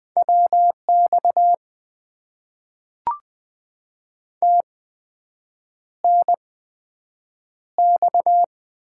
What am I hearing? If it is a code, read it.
Morse code, heard as a keyed tone, about 20 wpm, 700 Hz. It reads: W X T N X
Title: Top 90 QSO Elements - 8x Character Spacing - 20wpm